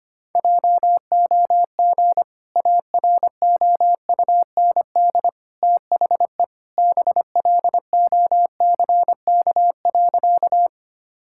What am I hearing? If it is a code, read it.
Morse code, heard as a keyed tone, about 25 wpm, 700 Hz. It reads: JOG AROUND THE BLOCK.
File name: Easy Sentences with Hints - 25wpm